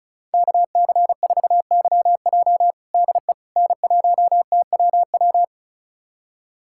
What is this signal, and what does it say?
Morse code, heard as a keyed tone, about 36 wpm, 690 Hz. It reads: KC4YJ DE N1TWW
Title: USA Callsign DE Callsign - 5 Characters - 35wpm